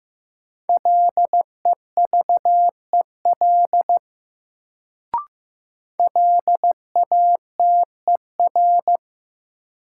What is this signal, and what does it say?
Morse code, heard as a keyed tone, about 15 wpm, 700 Hz. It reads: LEVEL LATER
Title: Sentences from Top 700 Words - New Word Drill - 15wpm